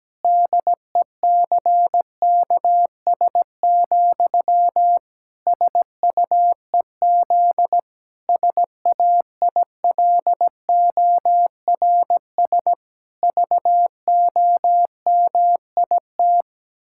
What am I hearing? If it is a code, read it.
Morse code, heard as a keyed tone, about 17 wpm, 700 Hz. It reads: DECKS, SUEZ SAILORS VOMIT
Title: Pangram Challenge v2 - 17wpm